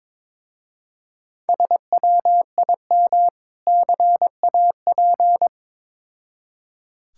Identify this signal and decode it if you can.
Morse code, heard as a keyed tone, about 22 wpm, 700 Hz. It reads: SWIM CAP
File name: practice / 22wpm / Difficult Words with Hints - 22wpm